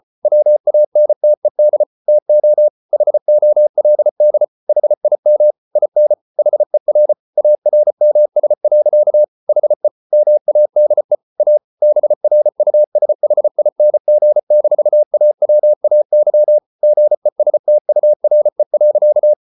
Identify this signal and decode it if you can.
Morse code, heard as a keyed tone, about 34 wpm, 600 Hz. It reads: WANTED TO HOLD HIM IN HER ARMS. HE MADE A BRUSHING-AWAY GESTURE.